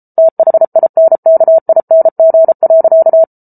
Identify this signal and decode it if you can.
Morse code, heard as a keyed tone, about 34 wpm, 650 Hz. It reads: THINKING.